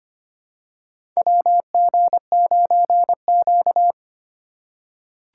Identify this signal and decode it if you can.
Morse code, heard as a keyed tone, about 25 wpm, 700 Hz. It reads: WG9Q
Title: USA Call Signs - 4 Characters - 3x Character Spacing - 25wpm